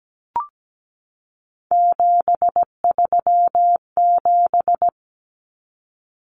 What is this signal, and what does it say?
Morse code, heard as a keyed tone, about 17 wpm, 700 Hz. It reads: 737
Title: Numbers - 3-Digit 17wpm